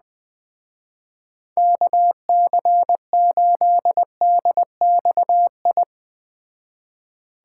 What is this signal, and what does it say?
Morse code, heard as a keyed tone, about 21 wpm, 700 Hz. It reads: KC8DXI